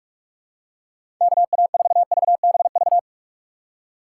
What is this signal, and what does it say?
Morse code, heard as a keyed tone, about 45 wpm, 700 Hz. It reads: KA4VBV
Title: USA Call Signs - 2x Character Spacing - 45wpm